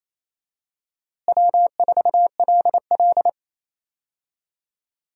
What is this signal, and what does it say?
Morse code, heard as a keyed tone, about 28 wpm, 700 Hz. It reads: W4LL
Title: USA Call Signs - 4 Characters - Repeated 3x - 28wpm